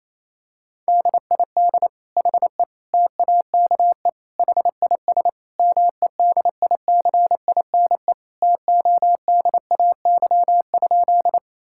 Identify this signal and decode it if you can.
Morse code, heard as a keyed tone, about 27 wpm, 700 Hz. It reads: DID HE TAKE HIS MEDICINE TODAY?
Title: Sentences from Top 1000 Words - Review Part 6 - 28wpm